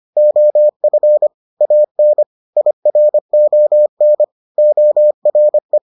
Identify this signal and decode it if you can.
Morse code, heard as a keyed tone, about 25 wpm, 600 Hz. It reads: OF AN IRON ORE